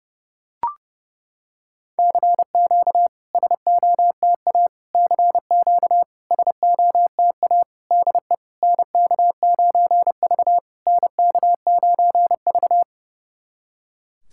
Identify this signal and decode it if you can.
Morse code, heard as a keyed tone, about 30 wpm, 700 Hz. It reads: CQ SOTA CQ SOTA DE NK9V NK9V
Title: SOTA QSO Practice 30wpm